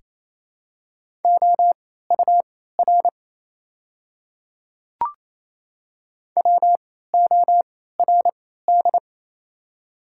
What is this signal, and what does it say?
Morse code, heard as a keyed tone, about 27 wpm, spaced as for 16 wpm, 700 Hz. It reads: OUR WORD
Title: 3x Character Spacing - Top 200 Words - 28wpm